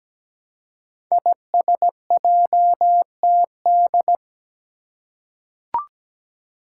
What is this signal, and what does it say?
Morse code, heard as a keyed tone, about 17 wpm, 700 Hz. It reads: ISJTD
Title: Five Challenging Letter-Number Combinations 17wpm